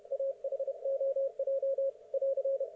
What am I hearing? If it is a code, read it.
Morse code, heard as a keyed tone, about 31 wpm, 530 Hz. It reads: AHOJ <AR>
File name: Archiv / CUC107 / ok1fhi cul107